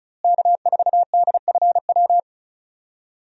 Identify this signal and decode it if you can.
Morse code, heard as a keyed tone, about 36 wpm, 700 Hz. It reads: K4DFW
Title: USA Call Signs - 5 Characters - Repeated 2x - 35wpm